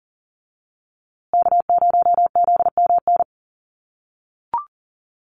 Morse code, heard as a keyed tone, about 40 wpm, 700 Hz. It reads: K0ZMN
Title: USA Call Signs Repeated 2x 40wpm